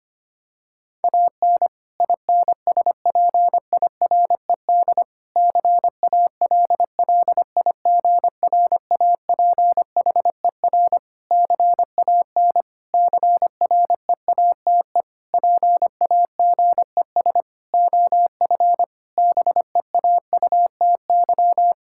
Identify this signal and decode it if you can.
Morse code, heard as a keyed tone, about 25 wpm, 700 Hz. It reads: AN INSPIRED CALLIGRAPHER CAN CREATE PAGES OF BEAUTY
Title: Pangram Challenge v2 - 25wpm